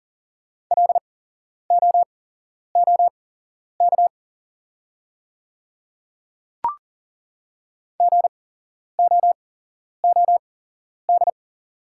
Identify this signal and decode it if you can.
Morse code, heard as a keyed tone, about 40 wpm, spaced as for 11 wpm, 690 Hz. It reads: LOOK GOOD
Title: 8x Character Spacing - Top 200 Words - 40wpm